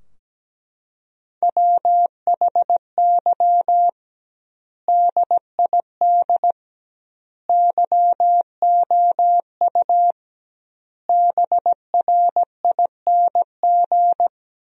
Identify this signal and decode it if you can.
Morse code, heard as a keyed tone, about 17 wpm, 700 Hz. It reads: WHY DID YOU BRING
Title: 2x Word-Spacing - Sentences from Top 500 Words - 17wpm